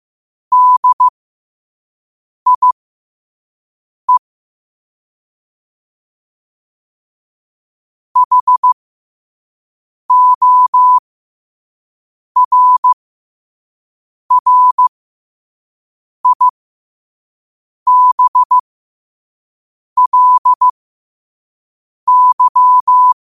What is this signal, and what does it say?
Morse code, heard as a keyed tone, about 15 wpm, spaced as for 5 wpm, 1000 Hz. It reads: DIE HORRIBLY
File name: qotdfiles / 5wpm